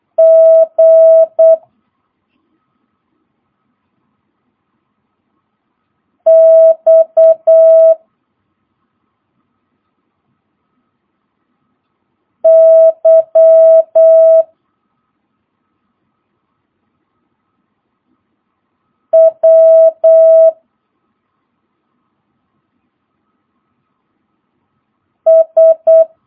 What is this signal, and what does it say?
Morse code, heard as a keyed tone, about 8 wpm, 650 Hz. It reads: G X Y W S
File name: jh8cbh / 2022cw